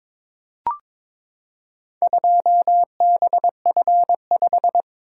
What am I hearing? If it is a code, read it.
Morse code, heard as a keyed tone, about 22 wpm, 700 Hz. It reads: 2BF5
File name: practice / 22wpm / Four Letter-Number Combinations - 22wpm